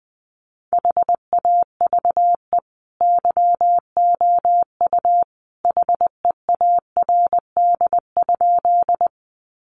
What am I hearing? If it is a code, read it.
Morse code, heard as a keyed tone, about 20 wpm, 700 Hz. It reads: HAVE YOU HEARD?
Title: Sentences from Top 400 Words 20wpm